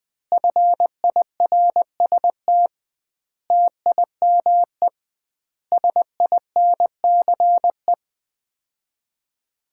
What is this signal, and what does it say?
Morse code, heard as a keyed tone, about 20 wpm, 700 Hz. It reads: FIRST TIME SINCE
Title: 2x Word-Spacing - Sets of 5 Words - 20wpm